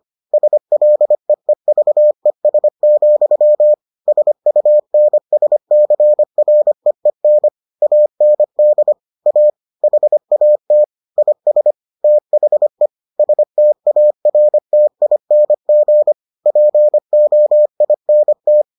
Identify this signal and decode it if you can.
Morse code, heard as a keyed tone, about 25 wpm, 600 Hz. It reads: SLEEVES, SUNSCREEN AND A HAT IS THE STARTING POINT